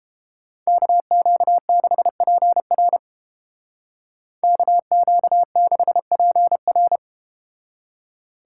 Morse code, heard as a keyed tone, about 32 wpm, 700 Hz. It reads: KQ6PR KQ6PR